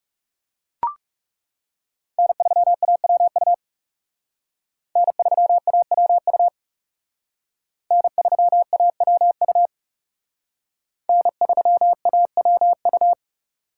Morse code, heard as a keyed tone, about 34 wpm, 700 Hz. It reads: N3AWU N3AWU N3AWU N3AWU
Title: Speed Racing - USA 1x3 Call Signs 30wpm